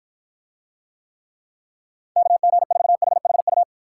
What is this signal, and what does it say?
Morse code, heard as a keyed tone, about 53 wpm, 700 Hz. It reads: KG4HHU